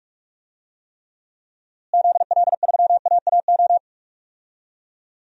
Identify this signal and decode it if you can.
Morse code, heard as a keyed tone, about 46 wpm, 700 Hz. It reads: ZL3AAO
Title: DXCC-100 Call Signs - 6 Characters - Repeated 2x - 45wpm